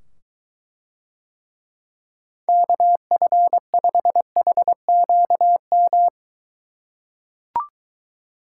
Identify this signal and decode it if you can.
Morse code, heard as a keyed tone, about 23 wpm, 700 Hz. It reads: KF5HQM